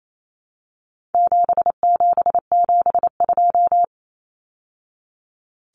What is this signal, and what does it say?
Morse code, heard as a keyed tone, about 28 wpm, 700 Hz. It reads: 7772
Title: Numbers - 4-Digit 28wpm